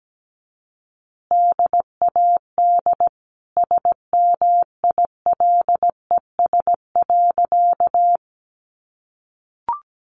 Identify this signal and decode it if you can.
Morse code, heard as a keyed tone, about 17 wpm, 700 Hz. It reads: DAD SMILES.
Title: Easy Phrases 17wpm